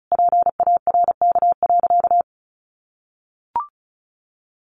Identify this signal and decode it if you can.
Morse code, heard as a keyed tone, about 34 wpm, 700 Hz. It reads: PARK.
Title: Sentences from Top 1300 Words 35wpm